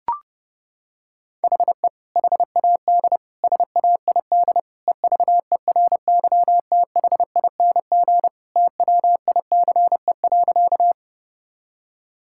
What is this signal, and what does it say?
Morse code, heard as a keyed tone, about 30 wpm, 700 Hz. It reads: HE HAD SAID EVERYTHING TWICE.
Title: Sentences from Top 1600 Words 30wpm